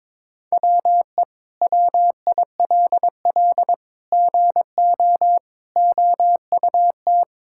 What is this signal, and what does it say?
Morse code, heard as a keyed tone, about 21 wpm, 700 Hz. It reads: WE WILL GO OUT
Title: Sentences from Top 1000 Words - Review Part 6 - 22wpm